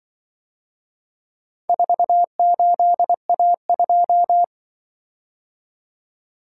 Morse code, heard as a keyed tone, about 24 wpm, 700 Hz. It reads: 48A2
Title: Speed-Racing - Four Letter-Number Combinations - 20wpm